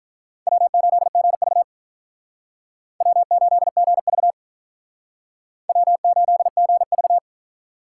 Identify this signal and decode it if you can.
Morse code, heard as a keyed tone, about 44 wpm, 700 Hz. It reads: W8GV W8GV W8GV